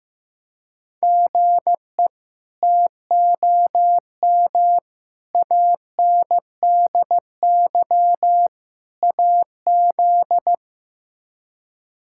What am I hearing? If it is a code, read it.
Morse code, heard as a keyed tone, about 15 wpm, 700 Hz. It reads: GE TOM ANDY AZ
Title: K1USN SST QSO Practice - 15wpm